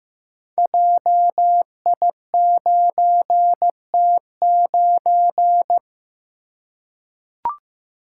Morse code, heard as a keyed tone, about 15 wpm, 700 Hz. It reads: JI9T9
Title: Code Groups - 5 Letter:Numbers - 15wpm